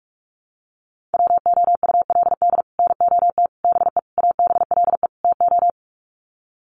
Morse code, heard as a keyed tone, about 44 wpm, 700 Hz. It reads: WOULD NOT BE ABLE TO
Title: Sets of 5 Words 45wpm